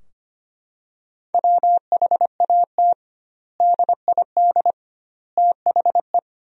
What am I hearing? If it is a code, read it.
Morse code, heard as a keyed tone, about 25 wpm, 700 Hz. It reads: WHAT DID THE
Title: 2x Word-Spacing - Sentences from Top 500 Words - 25wpm